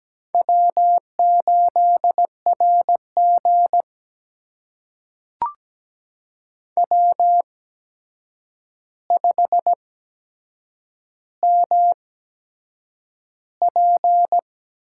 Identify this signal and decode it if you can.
Morse code, heard as a keyed tone, about 17 wpm, spaced as for 7 wpm, 700 Hz. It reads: W8RG W5MP
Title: USA Call Signs - 4 Characters - 8x Character Spacing - 17wpm